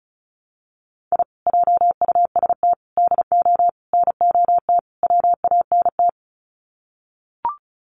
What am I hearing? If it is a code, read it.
Morse code, heard as a keyed tone, about 34 wpm, 700 Hz. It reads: I JUST DO NOT WANT
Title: Sets of 5 Words Encore2 35wpm